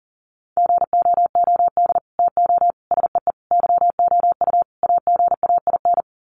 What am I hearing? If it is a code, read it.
Morse code, heard as a keyed tone, about 40 wpm, 700 Hz. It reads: GOOD TO SEE YOU AGAIN
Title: Sets of 5 Words Encore2 40wpm